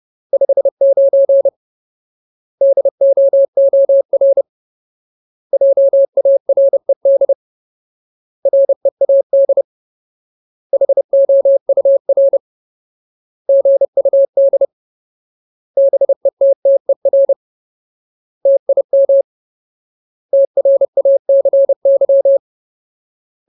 Morse code, heard as a keyed tone, about 30 wpm, 550 Hz. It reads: 59 DOOR JARED READ HOUR GUD BETTER TIM TRACY
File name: Ultimate Word List / 30WPM Ultimate